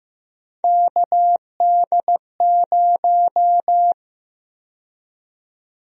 Morse code, heard as a keyed tone, about 15 wpm, 700 Hz. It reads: KD0
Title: USA Call Sign Prefixes - 15wpm